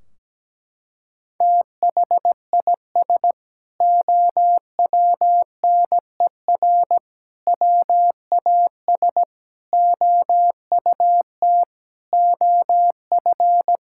Morse code, heard as a keyed tone, about 17 wpm, 700 Hz. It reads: THIS OWNER WAS OUT OF
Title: Sentences from Top 1600 Words 17wpm